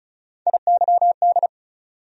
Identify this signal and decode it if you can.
Morse code, heard as a keyed tone, about 36 wpm, 700 Hz. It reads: IYD
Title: USA Call Sign Postfixes - 3 Letters 35wpm